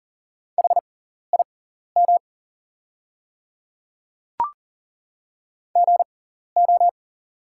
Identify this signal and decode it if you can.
Morse code, heard as a keyed tone, about 40 wpm, spaced as for 14 wpm, 700 Hz. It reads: HIM GO
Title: 6x Character Spacing - Top 100 Words - 40wpm